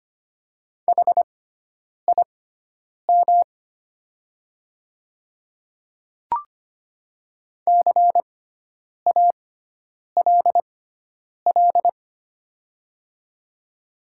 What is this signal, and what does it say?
Morse code, heard as a keyed tone, about 25 wpm, spaced as for 9 wpm, 700 Hz. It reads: HIM CALL
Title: 6x Character Spacing - Top 100 Words - 25wpm